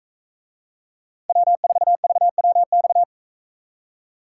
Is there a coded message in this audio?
Yes: Morse code from 1.3 to 3.0 s, heard as a keyed tone, about 44 wpm, 700 Hz. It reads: W4VWX